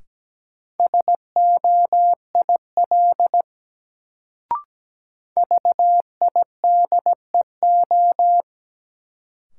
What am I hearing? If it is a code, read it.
Morse code, heard as a keyed tone, about 17 wpm, 700 Hz. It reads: SOIL VIDEO
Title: Sentences from Top 1000 Words - Review Part 4 - New Word Drill - 17wpm